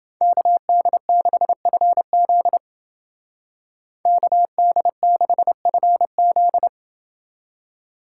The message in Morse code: KD6FZ KD6FZ

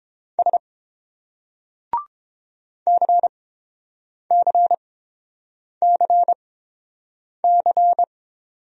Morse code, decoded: S CCCC

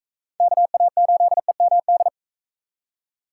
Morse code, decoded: KA8EMD

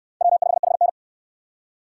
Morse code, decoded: W5HN

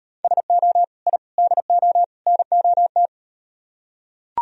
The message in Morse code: SO I DO NOT

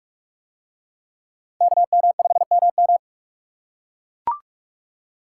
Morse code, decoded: KM5MM

